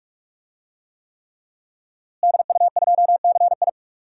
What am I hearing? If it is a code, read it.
Morse code, heard as a keyed tone, about 46 wpm, 700 Hz. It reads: DU2CI
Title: DXCC-100 Call Signs - 5 Characters - Repeated 3x - 45wpm